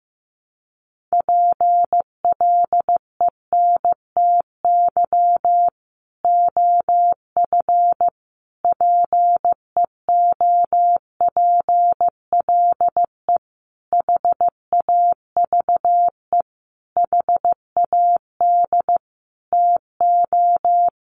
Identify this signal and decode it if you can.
Morse code, heard as a keyed tone, about 15 wpm, 700 Hz. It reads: PLENTY OF PEOPLE HAVE HAD TO
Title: Sentences from Top 1300 Words 15wpm